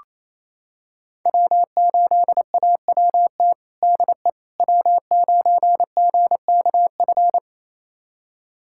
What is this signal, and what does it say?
Morse code, heard as a keyed tone, about 28 wpm, 700 Hz. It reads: W8AWT DE W9GKF
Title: USA Callsign DE Callsign - 5 Characters - 28wpm